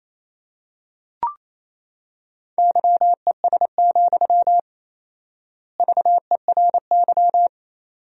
Morse code, heard as a keyed tone, about 27 wpm, 700 Hz. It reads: YES, VERY
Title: 4x Word-Spacing - Sentences from Top 500 Words - 28wpm